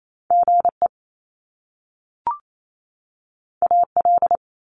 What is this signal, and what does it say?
Morse code, heard as a keyed tone, about 28 wpm, 700 Hz. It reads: GE AL
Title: Top Two Letter Combinations in Words 28wpm